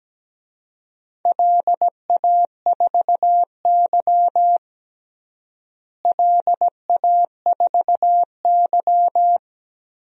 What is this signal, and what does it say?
Morse code, heard as a keyed tone, about 17 wpm, 700 Hz. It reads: LA4Y LA4Y
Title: DXCC-100 Call Signs - 4 Characters - Repeated 3x - 17wpm